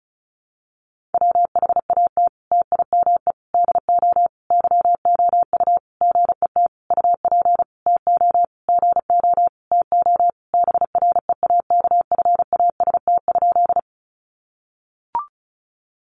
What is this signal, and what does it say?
Morse code, heard as a keyed tone, about 34 wpm, 700 Hz. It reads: WHAT TIME DO YOU GET UP TO GO TO BREAKFAST?
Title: Sentences from Top 1500 Words 35wpm